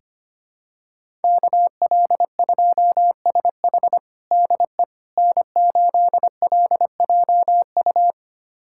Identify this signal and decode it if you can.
Morse code, heard as a keyed tone, about 25 wpm, 700 Hz. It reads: KL2SH DE N8LJU
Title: USA Callsign DE Callsign 25wpm